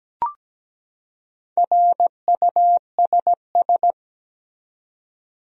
Morse code, heard as a keyed tone, about 17 wpm, 700 Hz. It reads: RUSS
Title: USA Common Names - Max 4 Letters - 17wpm